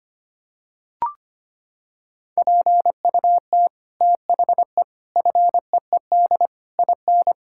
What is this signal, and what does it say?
Morse code, heard as a keyed tone, about 25 wpm, 700 Hz. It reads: PUT THE FEED IN
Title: Sentences from Top 1600 Words 25wpm